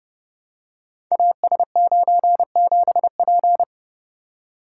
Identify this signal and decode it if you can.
Morse code, heard as a keyed tone, about 30 wpm, 700 Hz. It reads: AS97P
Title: Code Groups - 5 Letter:Numbers - 30wpm